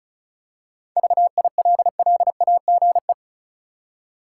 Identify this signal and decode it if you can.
Morse code, heard as a keyed tone, about 34 wpm, 700 Hz. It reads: VILLAGE
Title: Seven Letter Words - 35wpm